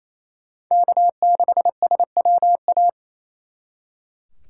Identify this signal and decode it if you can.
Morse code, heard as a keyed tone, about 35 wpm, 700 Hz. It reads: K6SWA
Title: Speed Racing - USA 1x3 Call Signs 28wpm